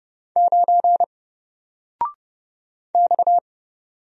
Morse code, heard as a keyed tone, about 30 wpm, 700 Hz. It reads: 9 X
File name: Single-Letter-Number-30wpm